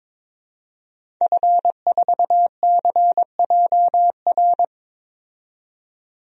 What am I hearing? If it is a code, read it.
Morse code, heard as a keyed tone, about 22 wpm, 700 Hz. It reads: F4CJR
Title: DXCC-100 Call Signs - 5 Characters - 22wpm